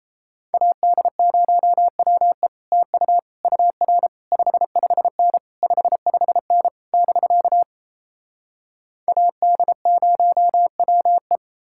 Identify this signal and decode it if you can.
Morse code, heard as a keyed tone, about 38 wpm, 700 Hz. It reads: AD0WE TU UR 55N 55N <BK> AD0WE